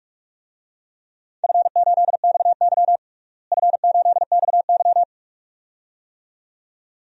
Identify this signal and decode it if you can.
Morse code, heard as a keyed tone, about 47 wpm, 700 Hz. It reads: F8XY F8XY
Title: DXCC-100 Call Signs - 4 Characters - Repeated 2x - 45wpm